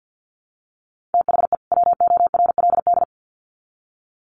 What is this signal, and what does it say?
Morse code, heard as a keyed tone, about 50 wpm, 700 Hz. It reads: THE WORLD
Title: Sets of 2 Words 50wpm